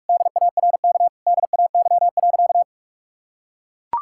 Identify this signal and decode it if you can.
Morse code, heard as a keyed tone, about 41 wpm, 700 Hz. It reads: DARK DAY.
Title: Sentences from Top 1000 Words - Review Part 5 - 45wpm